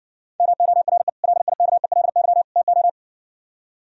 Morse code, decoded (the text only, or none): MORE LIKELY TO